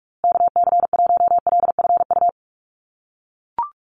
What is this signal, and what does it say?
Morse code, heard as a keyed tone, about 44 wpm, 700 Hz. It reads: KC1LFU
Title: USA Call Signs Repeated 2x 45wpm